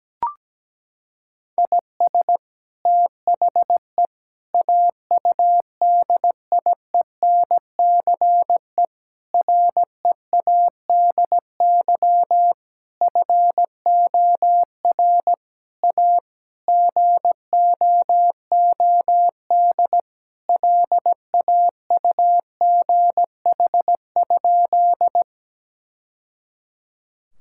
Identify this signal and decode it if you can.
Morse code, heard as a keyed tone, about 17 wpm, 700 Hz. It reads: IS THE AUDIENCE READY FOR A GOOD LAUGH?
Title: Sentences from Top 1000 Words - Review Part 5 - 17wpm